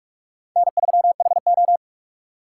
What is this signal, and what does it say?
Morse code, heard as a keyed tone, about 46 wpm, 700 Hz. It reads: N3HO
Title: USA Call Signs - 3x Character Spacing - 45wpm